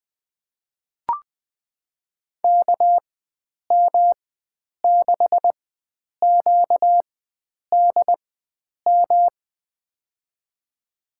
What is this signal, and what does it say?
Morse code, heard as a keyed tone, about 20 wpm, spaced as for 9 wpm, 700 Hz. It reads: KM6QDM
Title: USA Call Signs - 6 Characters - 4x Character Spacing - 20wpm